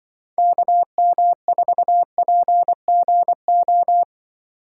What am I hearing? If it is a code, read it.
Morse code, heard as a keyed tone, about 25 wpm, 700 Hz. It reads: KM4PGO
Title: Speed Racing - USA 2x3 Call Signs 20wpm